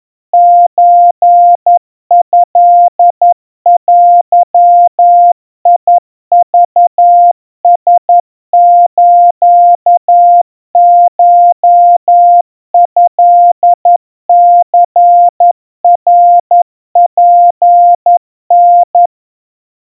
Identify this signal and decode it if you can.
Morse code (Japanese Wabun code), heard as a keyed tone, about 11 wpm, 700 Hz. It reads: ソトデクラスコトニナツタ
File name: kawa / Morse sample / kamyu 40 cpm 700 Hz Crispy